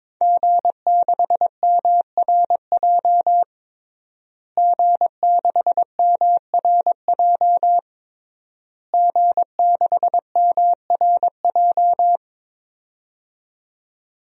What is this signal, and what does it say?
Morse code, heard as a keyed tone, about 21 wpm, 700 Hz. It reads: G6MRJ G6MRJ G6MRJ
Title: DXCC-100 Call Signs - 5 Characters - Repeated 3x - 22wpm